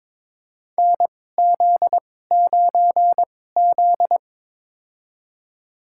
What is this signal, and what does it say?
Morse code, heard as a keyed tone, about 22 wpm, spaced as for 16 wpm, 700 Hz. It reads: NZ9Z
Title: USA Call Signs - 4 Characters - 2x Character Spacing - 22wpm